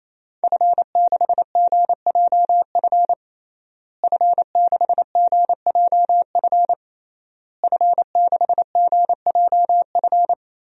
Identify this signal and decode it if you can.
Morse code, heard as a keyed tone, about 28 wpm, 700 Hz. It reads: F6GJF F6GJF F6GJF
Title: DXCC-100 Call Signs - 5 Characters - Repeated 3x - 28wpm